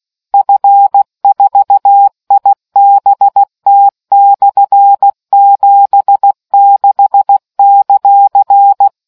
Morse code, heard as a keyed tone, about 16 wpm, 800 Hz. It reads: F4IBT/76;